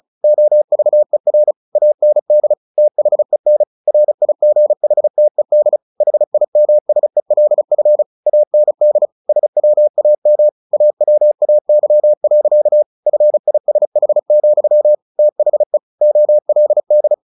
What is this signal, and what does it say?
Morse code, heard as a keyed tone, about 34 wpm, 600 Hz. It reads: OVER AND THEN RIGHTED HIMSELF AND SWAM AWAY. FISH, THE OLD
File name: the old man and the sea 2 35wpm